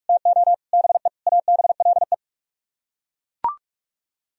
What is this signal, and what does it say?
Morse code, heard as a keyed tone, about 42 wpm, 700 Hz. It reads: TO BE ABLE